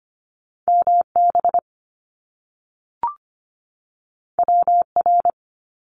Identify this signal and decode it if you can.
Morse code, heard as a keyed tone, about 25 wpm, 700 Hz. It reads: MB WR
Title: Top Two Letter Combinations in Words 25wpm